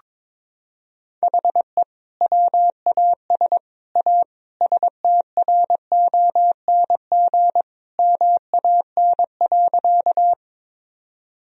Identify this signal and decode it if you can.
Morse code, heard as a keyed tone, about 22 wpm, 700 Hz. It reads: HE WAS A STRONG MAN.
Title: Sentences from Top 1000 Words - Review Part 3 - 22wpm